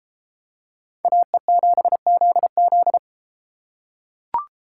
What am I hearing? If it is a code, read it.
Morse code, heard as a keyed tone, about 33 wpm, 700 Hz. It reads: AE7ZZ